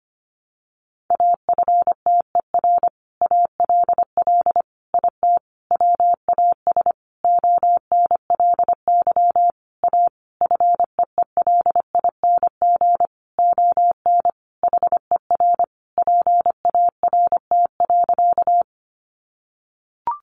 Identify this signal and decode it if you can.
Morse code, heard as a keyed tone, about 25 wpm, 700 Hz. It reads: AFTER ALL IT WAS ONLY A FEELING ON HER PART.
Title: Sentences from Top 1200 Words 25wpm